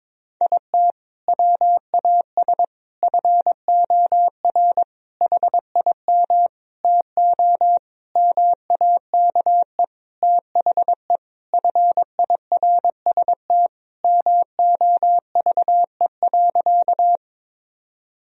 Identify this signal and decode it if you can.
Morse code, heard as a keyed tone, about 22 wpm, 700 Hz. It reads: IT WAS FOR HIM TO MAKE THE FIRST MOVE.
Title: Sentences from Top 1000 Words - Review Part 1 - 22wpm